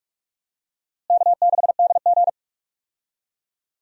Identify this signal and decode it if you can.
Morse code, heard as a keyed tone, about 46 wpm, 700 Hz. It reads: K6DG
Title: USA Call Signs - 4 Characters - Repeated 3x - 45wpm